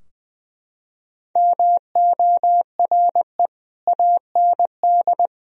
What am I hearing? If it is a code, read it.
Morse code, heard as a keyed tone, about 20 wpm, 700 Hz. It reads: MORE AND
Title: Sentences from Top 1000 Words - Review Part 3 - 20wpm